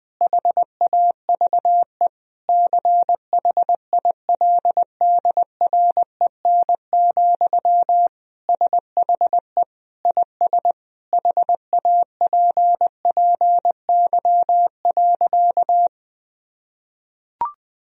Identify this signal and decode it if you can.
Morse code, heard as a keyed tone, about 20 wpm, 700 Hz. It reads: HAVE CHILDREN, SHE IS HAPPY.
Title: Sentences from Top 1000 Words - Review Part 3 - 20wpm